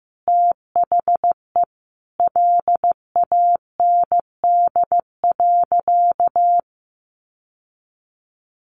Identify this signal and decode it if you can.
Morse code, heard as a keyed tone, about 15 wpm, 700 Hz. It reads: THE LAND.
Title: Sentences from Top 1500 Words 15wpm